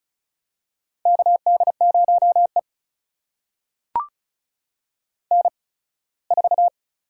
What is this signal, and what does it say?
Morse code, heard as a keyed tone, about 36 wpm, spaced as for 21 wpm, 700 Hz. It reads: KD0E N4
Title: USA Call Signs - 4 Characters - 8x Character Spacing - 35wpm